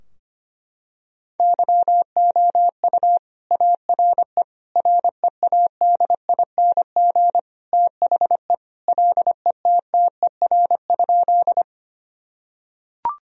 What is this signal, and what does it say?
Morse code, heard as a keyed tone, about 25 wpm, 700 Hz. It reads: YOU ARE READING THE LETTER?
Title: Sentences from Top 1000 Words - Review Part 3 - 25wpm